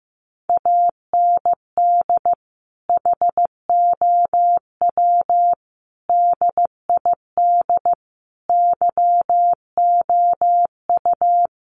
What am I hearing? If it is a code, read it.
Morse code, heard as a keyed tone, about 15 wpm, 700 Hz. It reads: AND HOW DID YOU
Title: Sentences from Top 100 Words 15wpm